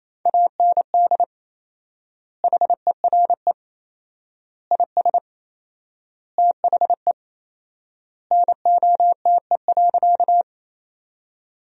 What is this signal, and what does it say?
Morse code, heard as a keyed tone, about 28 wpm, 700 Hz. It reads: AND HERE IS THE NOTE.
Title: 4x Word-Spacing - Sentences from Top 500 Words - 28wpm